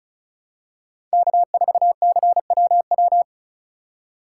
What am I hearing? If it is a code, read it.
Morse code, heard as a keyed tone, about 34 wpm, 700 Hz. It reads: K4CWW
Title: USA Call Signs - 5 Characters - 2x Character Spacing - 35wpm